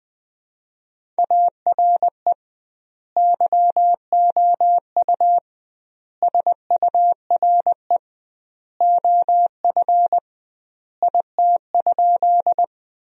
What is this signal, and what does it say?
Morse code, heard as a keyed tone, about 20 wpm, 700 Hz. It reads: ARE YOU SURE OF IT?
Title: 2x Word-Spacing - Sentences from Top 300 Words - 20wpm